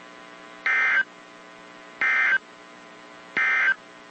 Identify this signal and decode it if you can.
Emergency Alert System (SAME) tones, heard as harsh buzzing bursts. It is heard from 0.5 to 3.8 s.
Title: WZ2523 - NOAA Weather Radio